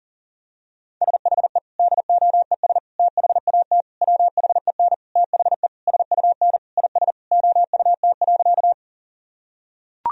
Morse code, heard as a keyed tone, about 40 wpm, 700 Hz. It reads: SHE DOES THAT WHEN THE SUN IS OUT.
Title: Sentences from Top 1000 Words - Review Part 1 - 40wpm